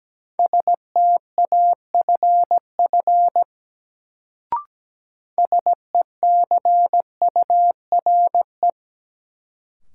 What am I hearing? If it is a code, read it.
Morse code, heard as a keyed tone, about 17 wpm, 700 Hz. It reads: STAFF SECURE
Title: Sentences from Top 1200 Words - New Word Drill - 17wpm